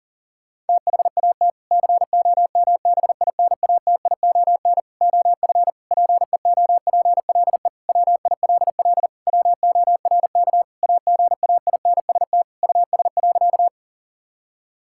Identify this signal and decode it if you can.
Morse code, heard as a keyed tone, about 40 wpm, 700 Hz. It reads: THAT COMBINATION OF PEOPLE WILL WORK AGAINST US.
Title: Sentences from Top 1000 Words - Review Part 6 - 40wpm